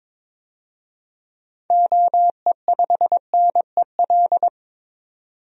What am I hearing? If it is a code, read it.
Morse code, heard as a keyed tone, about 21 wpm, 700 Hz. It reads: OE5NEL
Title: DXCC-100 Call Signs - 6 Characters - 22wpm